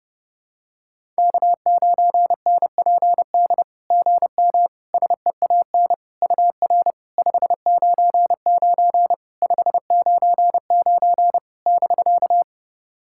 Morse code, heard as a keyed tone, about 30 wpm, 690 Hz. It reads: K9NPD GM SEAN UR 599 599 <BK>
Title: SOTA QSO Practice 30wpm